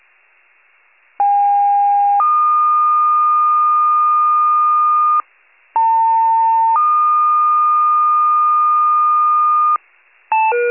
Two-tone page, heard as a one-second tone, then a three-second tone, about 810 then 1220 Hz, about 900 then 1220 Hz. It is heard from 1.2 to 9.8 s.
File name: pager / 78 RESCUE / wcalert